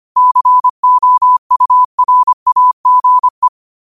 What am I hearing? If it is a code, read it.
Morse code, heard as a keyed tone, about 25 wpm, 1000 Hz. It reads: COURAGE
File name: qotdfiles / 25wpm